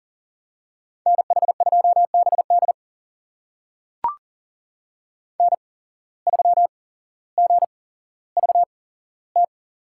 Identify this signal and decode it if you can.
Morse code, heard as a keyed tone, about 40 wpm, spaced as for 17 wpm, 700 Hz. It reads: NH2BD N3GVT
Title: USA Call Signs - 5 Characters - 8x Character Spacing - 40wpm